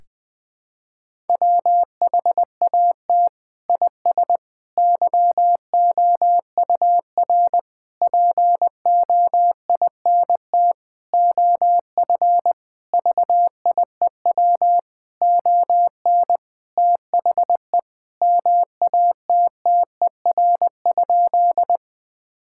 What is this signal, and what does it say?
Morse code, heard as a keyed tone, about 20 wpm, 700 Hz. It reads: WHAT IS YOUR POINT OF VIEW ON THE MATTER?
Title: Sentences from Top 1000 Words - Review Part 6 - 20wpm